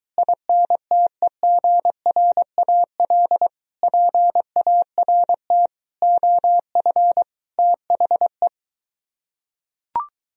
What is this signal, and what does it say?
Morse code, heard as a keyed tone, about 23 wpm, 700 Hz. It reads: INTEGRAL PART OF THE